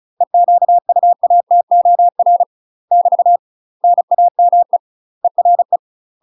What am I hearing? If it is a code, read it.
Morse code, heard as a keyed tone, about 36 wpm, 700 Hz. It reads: EQUATOR <BT> NAME ERE
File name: UK MASTER QSOs TRAINING / 35WPM / UK MASTER QSOs 0020